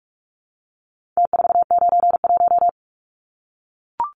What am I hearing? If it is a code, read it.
Morse code, heard as a keyed tone, about 44 wpm, 700 Hz. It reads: T491
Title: Numbers - Contest Style - 4-Digit 45wpm